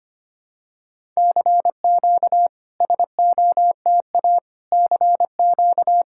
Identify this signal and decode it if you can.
Morse code, heard as a keyed tone, about 25 wpm, 700 Hz. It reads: CQ SOTA CQ
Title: SOTA QSO Practice 25wpm